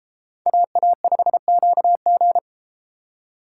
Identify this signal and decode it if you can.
Morse code, heard as a keyed tone, about 33 wpm, 700 Hz. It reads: AA5QG